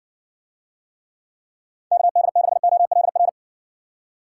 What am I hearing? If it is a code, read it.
Morse code, heard as a keyed tone, about 60 wpm, 700 Hz. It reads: KD6OLR